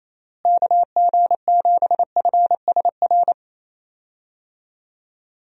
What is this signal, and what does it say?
Morse code, heard as a keyed tone, about 28 wpm, 700 Hz. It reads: KG7FSR
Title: USA Call Signs - 6 Characters - Repeated 2x - 28wpm